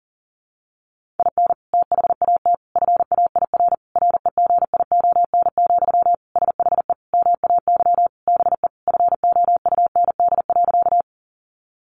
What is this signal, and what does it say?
Morse code, heard as a keyed tone, about 40 wpm, 700 Hz. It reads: IN THAT FAIR REGION, SHE MAY BE FOUND.
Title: Sentences from Top 1500 Words 40wpm